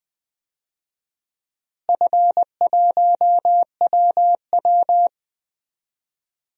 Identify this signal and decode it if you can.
Morse code, heard as a keyed tone, about 20 wpm, 700 Hz. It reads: F1WW
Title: DXCC-100 Call Signs - 4 Characters - Repeated 3x - 20wpm